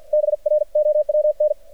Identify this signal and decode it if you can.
Morse code, heard as a keyed tone, about 47 wpm, 600 Hz. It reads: BROWN
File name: archived / CRSnet